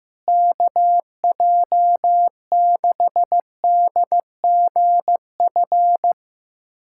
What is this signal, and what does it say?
Morse code, heard as a keyed tone, about 15 wpm, 700 Hz. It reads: KJ6DGF
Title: USA Callsign DE Callsign - 6 Characters - 15wpm